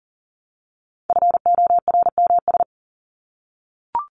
Morse code, heard as a keyed tone, about 40 wpm, 700 Hz. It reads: FORMS
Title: ES:S Suffix - Top 100 Words with Base Words 40wpm